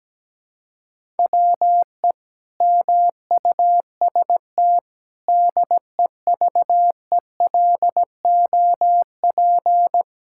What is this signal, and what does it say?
Morse code, heard as a keyed tone, about 17 wpm, 700 Hz. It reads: WE MUST DEVELOP
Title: Sentences from Top 1000 Words - Review Part 5 - 17wpm